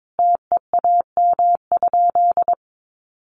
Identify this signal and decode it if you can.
Morse code, heard as a keyed tone, about 22 wpm, 700 Hz. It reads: TEAM?
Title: Sentences from Top 1500 Words 22wpm